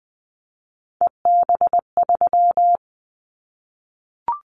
Morse code, heard as a keyed tone, about 20 wpm, 700 Hz. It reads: EB3
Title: Callsign Prefixes for Top 25 DXCC - 20wpm